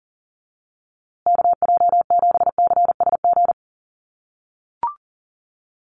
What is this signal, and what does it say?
Morse code, heard as a keyed tone, about 40 wpm, 700 Hz. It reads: KJ7CSG
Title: USA Call Signs Repeated 2x 40wpm